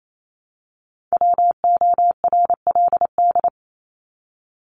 Morse code, heard as a keyed tone, about 28 wpm, 700 Hz. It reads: WORLD
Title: Top 200 Words 28wpm